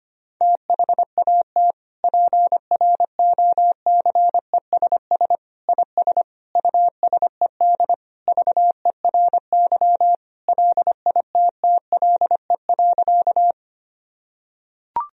Morse code, heard as a keyed tone, about 25 wpm, 700 Hz. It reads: THAT PROCESS IS USED VERY LITTLE.
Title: Sentences from Top 1000 Words - Review Part 3 - 25wpm